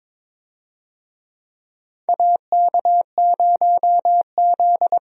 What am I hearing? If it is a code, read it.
Morse code, heard as a keyed tone, about 21 wpm, 700 Hz. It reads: AK0Z
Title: USA Call Signs - 4 Characters - Repeated 2x - 22wpm